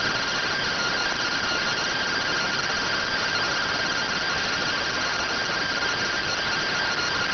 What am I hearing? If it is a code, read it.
Two-tone teleprinter signal, about 170 Hz shift.